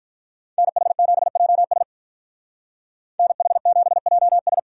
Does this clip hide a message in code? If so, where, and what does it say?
Yes: Morse code from 0.6 to 4.6 s, heard as a keyed tone, about 46 wpm, 700 Hz. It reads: NH7JS NH7JS